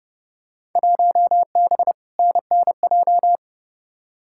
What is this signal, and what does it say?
Morse code, heard as a keyed tone, about 31 wpm, 700 Hz. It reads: 1B NNJ
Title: Field Day Exchanges - 30wpm